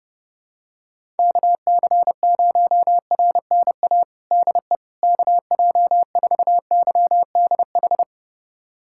Morse code, heard as a keyed tone, about 32 wpm, 700 Hz. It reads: KC0RNA DE KJ4YDH
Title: USA Callsign DE Callsign 30wpm